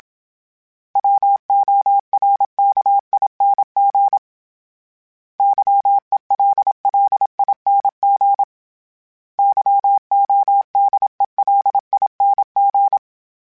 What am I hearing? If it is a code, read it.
Morse code, heard as a keyed tone, about 27 wpm, 800 Hz. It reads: WORKING YELLING YODELING
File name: Special INGs-mp3s / Special INGs-mp3s / Sngl Ing @25